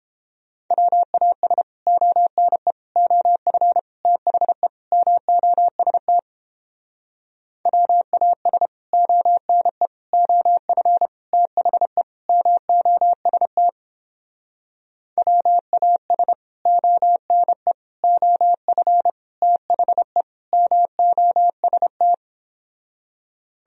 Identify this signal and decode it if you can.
Morse code, heard as a keyed tone, about 29 wpm, 700 Hz. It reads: WAS ONE OF THE MOST WAS ONE OF THE MOST WAS ONE OF THE MOST